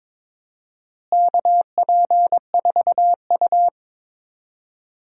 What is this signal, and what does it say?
Morse code, heard as a keyed tone, about 22 wpm, 700 Hz. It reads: KP4U
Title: DXCC-100 Call Signs - 4 Characters - Repeated 2x - 22wpm